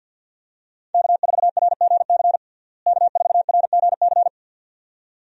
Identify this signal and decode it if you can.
Morse code, heard as a keyed tone, about 50 wpm, 700 Hz. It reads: K4RGC K4RGC
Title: USA Call Signs - 5 Characters - Repeated 2x - 50wpm